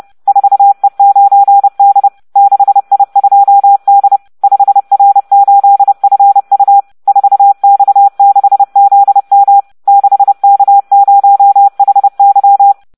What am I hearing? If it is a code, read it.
Morse code, heard as a keyed tone, about 30 wpm, 800 Hz. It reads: 4E9D 6I2D 5R8FU 4X6ZM 6K0HY